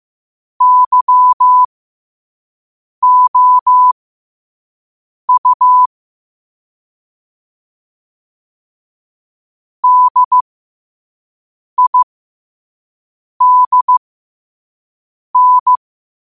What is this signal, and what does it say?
Morse code, heard as a keyed tone, about 15 wpm, spaced as for 5 wpm, 1000 Hz. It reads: YOU DIDN
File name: qotdfiles / 5wpm